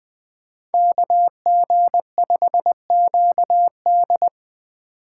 Morse code, heard as a keyed tone, about 20 wpm, 700 Hz. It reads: KG5QD
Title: Speed Racing - USA 2x2 Call Signs 17wpm